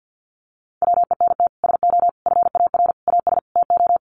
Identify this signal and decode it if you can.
Morse code, heard as a keyed tone, about 50 wpm, 700 Hz. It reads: WENT SO FAR AS TO
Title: Sets of 5 Words Encore 50wpm